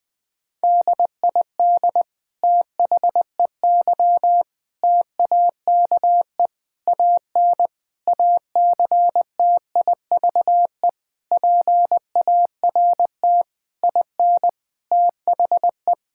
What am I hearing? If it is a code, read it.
Morse code, heard as a keyed tone, about 20 wpm, 700 Hz. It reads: DID THEY TAKE AN ACTIVE PART IN THE
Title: Sentences from Top 1000 Words - Review Part 6 - 20wpm